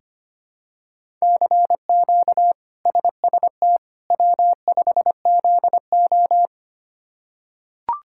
Morse code, heard as a keyed tone, about 25 wpm, 700 Hz. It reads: CQ SST W5ZO
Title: K1USN SST QSO Practice - 25wpm